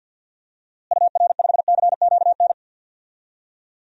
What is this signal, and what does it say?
Morse code, heard as a keyed tone, about 51 wpm, 700 Hz. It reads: UR5CQN
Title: DXCC-100 Call Signs - 6 Characters - Repeated 3x - 50wpm